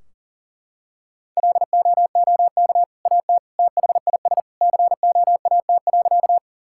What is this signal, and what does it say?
Morse code, heard as a keyed tone, about 40 wpm, 700 Hz. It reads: LOOK AT THIS COAT.
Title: Sentences from Top 1600 Words 40wpm